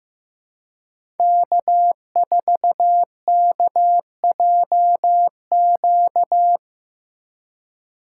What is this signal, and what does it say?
Morse code, heard as a keyed tone, about 15 wpm, 700 Hz. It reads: K4KJQ
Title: USA Call Signs - 5 Characters - 4x Character Spacing - 15wpm